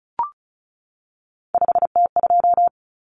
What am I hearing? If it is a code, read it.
Morse code, heard as a keyed tone, about 34 wpm, 700 Hz. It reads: 5T2
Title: Common Cut Numbers - 3-Digit 35wpm